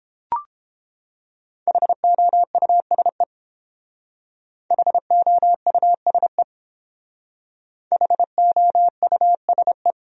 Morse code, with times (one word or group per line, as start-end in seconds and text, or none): 1.7-3.2 HOUSE
4.7-6.4 HOUSE
7.9-9.9 HOUSE